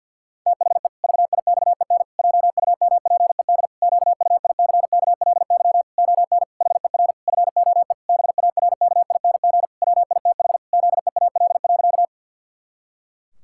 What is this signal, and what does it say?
Morse code, heard as a keyed tone, about 52 wpm, 700 Hz. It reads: THE VIXEN JUMPED QUICKLY ON HER FOE BARKING WITH ZEAL.
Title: Pangram Challenge v2 - 50wpm